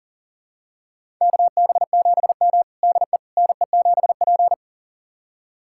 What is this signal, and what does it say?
Morse code, heard as a keyed tone, about 40 wpm, 700 Hz. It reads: KB7M DE NE7P
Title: USA Callsign DE Callsign - 4 Characters - 40wpm